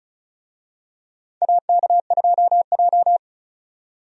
Morse code, heard as a keyed tone, about 36 wpm, 700 Hz. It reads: AK2J
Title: USA Call Signs - 4 Characters - 35wpm